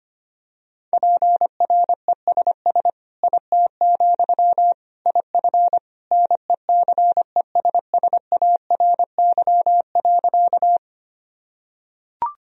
Morse code, heard as a keyed tone, about 25 wpm, 700 Hz. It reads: PRESS IT, IF NECESSARY.
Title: Sentences from Top 1000 Words - Review Part 1 - 25wpm